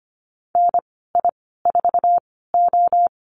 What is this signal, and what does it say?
Morse code, heard as a keyed tone, about 25 wpm, spaced as for 16 wpm, 700 Hz. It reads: NI4O
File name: Lesson 15g - Number 5 - Call Signs 25wpm